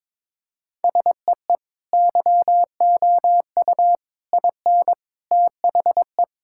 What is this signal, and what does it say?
Morse code, heard as a keyed tone, about 22 wpm, 700 Hz. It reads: SEE YOU IN THE
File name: practice / 22wpm / Sentences with 3-Word Phrases - 22wpm